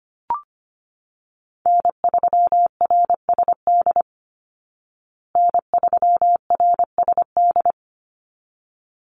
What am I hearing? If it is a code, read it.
Morse code, heard as a keyed tone, about 25 wpm, 700 Hz. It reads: N3RSD N3RSD
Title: Call Signs Repeated 3x - 25wpm